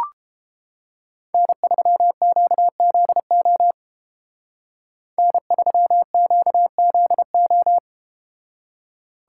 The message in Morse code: N3QZO N3QZO

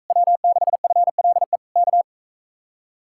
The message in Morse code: W6FLE K

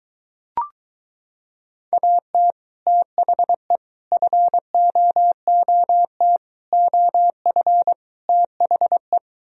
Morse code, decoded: AT THE FOOT OF THE

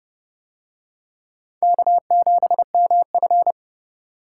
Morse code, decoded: K7MF

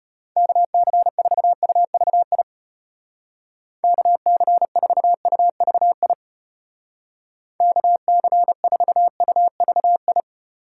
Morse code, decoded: KC4UVI KC4UVI KC4UVI

